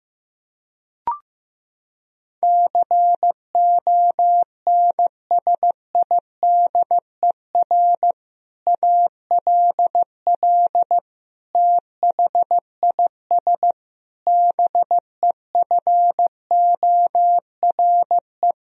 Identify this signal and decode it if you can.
Morse code, heard as a keyed tone, about 15 wpm, 700 Hz. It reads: CONSIDER ALL THIS BEFORE
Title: Sentences from Top 1000 Words - Review Part 4 - 15wpm